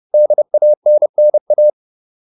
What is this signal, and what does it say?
Morse code, heard as a keyed tone, about 30 wpm, 600 Hz. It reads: DANNA
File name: QSO406 30